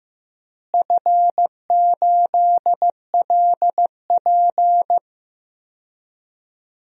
Morse code, heard as a keyed tone, about 15 wpm, 700 Hz. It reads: F8LP
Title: DXCC-100 Call Signs - 4 Characters - Repeated 2x - 15wpm